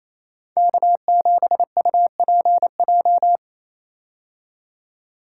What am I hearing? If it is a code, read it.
Morse code, heard as a keyed tone, about 27 wpm, 700 Hz. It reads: K7UPJ
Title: K1USN SST QSO Practice - 28wpm